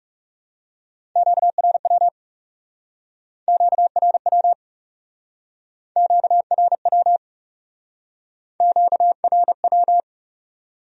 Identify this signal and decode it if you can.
Morse code, heard as a keyed tone, about 39 wpm, 700 Hz. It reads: QRW QRW QRW QRW